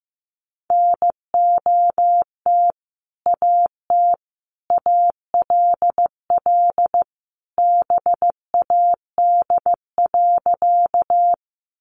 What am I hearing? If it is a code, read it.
Morse code, heard as a keyed tone, about 15 wpm, 700 Hz. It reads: NOT AT ALL BAD.
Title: Sentences from Top 700 Words 15wpm